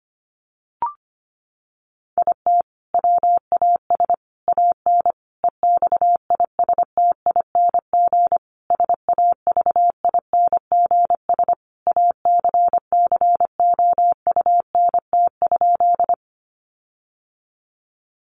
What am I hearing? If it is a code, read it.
Morse code, heard as a keyed tone, about 25 wpm, 700 Hz. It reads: IT WAS AN EXISTING SAVINGS ACCOUNT?
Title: Sentences from Top 1500 Words 25wpm